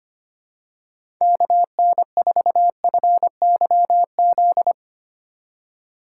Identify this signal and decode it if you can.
Morse code, heard as a keyed tone, about 25 wpm, 700 Hz. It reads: KN4FYZ
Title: USA Call Signs - 2x Character Spacing - 25wpm